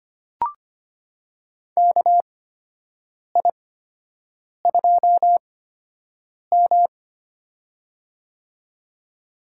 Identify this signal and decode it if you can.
Morse code, heard as a keyed tone, about 25 wpm, spaced as for 7 wpm, 700 Hz. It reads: KI2M
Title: USA Call Signs - 4 Characters - 8x Character Spacing - 25wpm